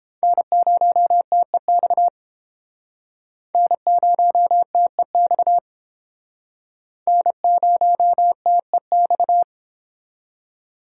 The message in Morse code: N0TEX N0TEX N0TEX

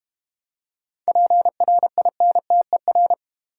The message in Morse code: PRINTER

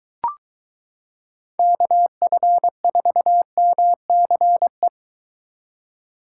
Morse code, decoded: KF4MCE

